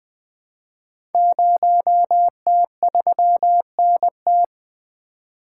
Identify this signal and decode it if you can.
Morse code, heard as a keyed tone, about 20 wpm, 700 Hz. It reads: 0T3NT
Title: Code Groups - 5 Letter:Numbers - 20wpm